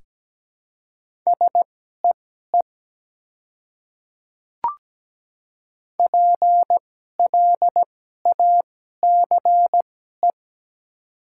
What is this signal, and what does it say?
Morse code, heard as a keyed tone, about 17 wpm, spaced as for 12 wpm, 700 Hz. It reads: SEE PLACE
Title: 2x Character Spacing - Top 300 Words - 17wpm